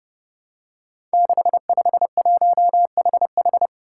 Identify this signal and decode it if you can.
Morse code, heard as a keyed tone, about 28 wpm, 700 Hz. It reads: 651HH
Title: Five Challenging Letter-Number Combinations 30wpm